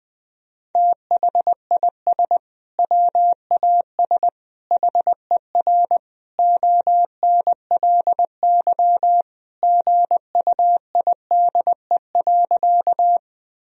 Morse code, heard as a keyed tone, about 20 wpm, 700 Hz. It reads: THIS WAS HER ONLY GUIDE.
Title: Sentences from Top 1000 Words - Review Part 3 - 20wpm